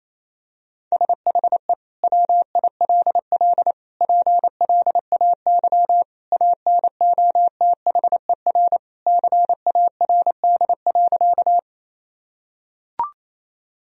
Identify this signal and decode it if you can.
Morse code, heard as a keyed tone, about 28 wpm, 700 Hz. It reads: SHE WILL PLAY ANOTHER CARD.
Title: Sentences from Top 1000 Words - Review Part 4 - 28wpm